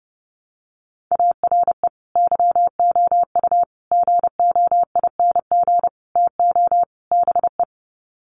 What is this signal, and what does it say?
Morse code, heard as a keyed tone, about 30 wpm, 700 Hz. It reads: ARE YOU GOING TO BE
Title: Sets of 5 Words Encore 30wpm